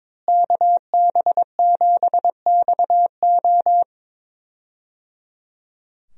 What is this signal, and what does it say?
Morse code, heard as a keyed tone, about 22 wpm, 700 Hz. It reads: KB7XO
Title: USA Call Signs - 5 Characters - Repeated 3x - 22wpm